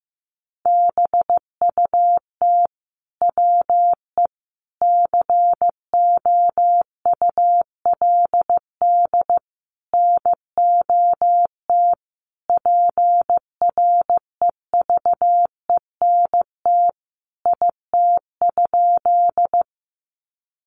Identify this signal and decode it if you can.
Morse code, heard as a keyed tone, about 15 wpm, 700 Hz. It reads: BUT WE COULD NOT PREVENT IT?
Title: Sentences from Top 1100 Words 15wpm